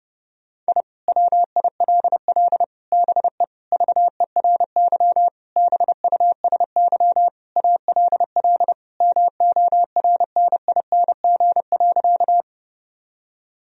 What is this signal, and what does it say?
Morse code, heard as a keyed tone, about 30 wpm, 700 Hz. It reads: I WILL BE VERY BUSY ALL MORNING.
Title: Sentences from Top 1000 Words - Review Part 3 - 30wpm